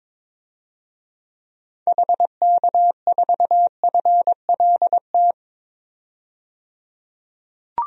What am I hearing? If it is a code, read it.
Morse code, heard as a keyed tone, about 22 wpm, 700 Hz. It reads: HK4FLT
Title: DXCC-100 Call Signs - 6 Characters - 22wpm